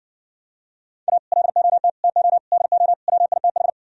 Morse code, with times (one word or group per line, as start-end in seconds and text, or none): 1.1-1.2 A
1.3-1.9 LOT
2.0-2.4 TO
2.5-2.9 DO
3.1-3.7 WITH